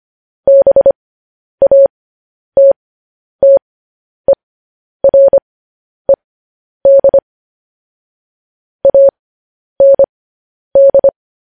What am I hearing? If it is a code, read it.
Morse code, heard as a keyed tone, about 25 wpm, spaced as for 10 wpm, 550 Hz. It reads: BATTERED AND
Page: news headlines in morse code at 10 wpm